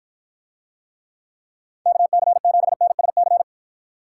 Morse code, heard as a keyed tone, about 53 wpm, 700 Hz. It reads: KC7NSC